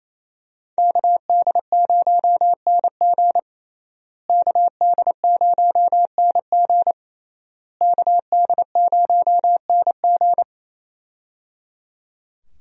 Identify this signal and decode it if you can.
Morse code, heard as a keyed tone, about 27 wpm, 700 Hz. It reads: KD0NG KD0NG KD0NG
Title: USA Call Signs - 5 Characters - Repeated 3x - 28wpm